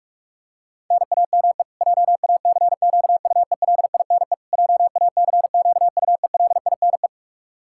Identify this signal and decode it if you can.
Morse code, heard as a keyed tone, about 41 wpm, 700 Hz. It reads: NAME JACQUELINE JACQUELINE
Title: Practice QSO - Exchange 1 - Beginning - 45wpm